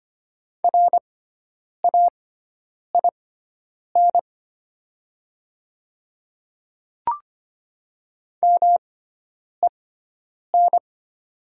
Morse code, heard as a keyed tone, about 25 wpm, spaced as for 9 wpm, 700 Hz. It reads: RAIN MEN
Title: 6x Character Spacing - Top 300 Words - 25wpm